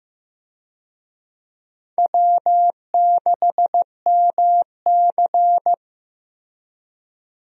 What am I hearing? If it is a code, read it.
Morse code, heard as a keyed tone, about 16 wpm, 700 Hz. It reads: W6MC
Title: USA Call Signs - 4 Characters - Repeated 2x - 15wpm